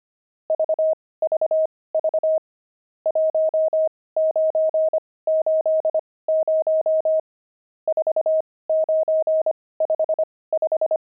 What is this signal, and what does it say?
Morse code, heard as a keyed tone, about 25 wpm, spaced as for 18 wpm, 630 Hz. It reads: VVV 1980 4955
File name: pangram-fs-18-cs-25-nums8